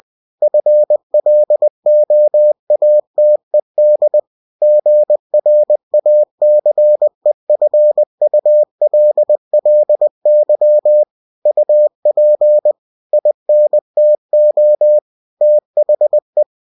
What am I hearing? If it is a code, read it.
Morse code, heard as a keyed tone, about 20 wpm, 600 Hz. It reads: FLOATED GRACEFULLY UP INTO THE
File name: cw-books / winnie-the-pooh / winnie-20wpm-page-0006